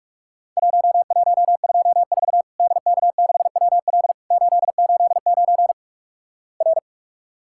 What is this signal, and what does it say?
Morse code, heard as a keyed tone, about 46 wpm, 700 Hz. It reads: 1124 DK6WL 889 R
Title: Worked All Europe DX Contest - QTC Exchange - 45wpm